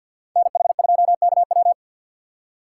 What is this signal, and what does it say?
Morse code, heard as a keyed tone, about 52 wpm, 700 Hz. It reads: NH2KW
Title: USA Call Signs - 5 Characters - 6x Character Spacing - 50wpm